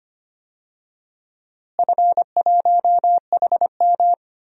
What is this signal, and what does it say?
Morse code, heard as a keyed tone, about 25 wpm, 700 Hz. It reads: F1HM
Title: DXCC-100 Call Signs - 4 Characters - Repeated 2x - 25wpm